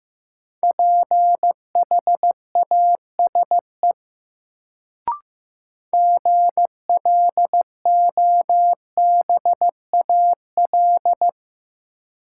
Morse code, heard as a keyed tone, about 15 wpm, 700 Hz. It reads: PHASE GLOBAL
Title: Sentences from Top 1200 Words - New Word Drill - 15wpm